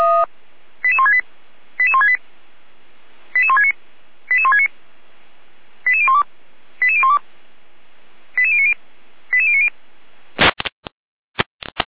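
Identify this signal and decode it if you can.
Five-tone selective call (ZVEI), heard as quick runs of five tones, tones of about 70 ms. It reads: 88168 88168 88169 88169 88012 88012 88090 88090
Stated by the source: ZVEI five-tone selective call 88365 received via SDR.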